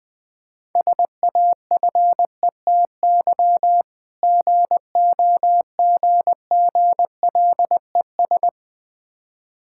Morse code, heard as a keyed tone, about 20 wpm, 700 Hz. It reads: SAFETY GOGGLES
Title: Difficult Words with Hints - 20wpm